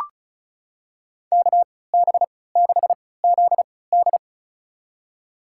Morse code, heard as a keyed tone, about 34 wpm, spaced as for 20 wpm, 700 Hz. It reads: KB6ZD
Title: USA Call Signs - 5 Characters - 3x Character Spacing - 35wpm